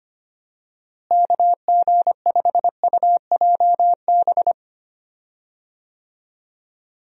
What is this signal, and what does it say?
Morse code, heard as a keyed tone, about 25 wpm, 700 Hz. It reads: KG5UJB
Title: USA Call Signs - 6 Characters - Repeated 3x - 25wpm